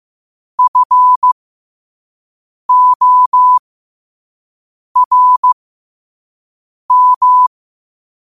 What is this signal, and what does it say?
Morse code, heard as a keyed tone, about 15 wpm, spaced as for 5 wpm, 1000 Hz. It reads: FORM
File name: qotdfiles / 5wpm